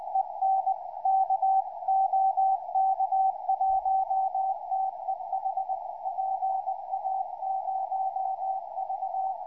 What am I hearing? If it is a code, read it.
Morse code, heard as a keyed tone, about 20 wpm, 760 Hz, with radio noise. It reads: EN K OK1TEH PA0V
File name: pa0v 2m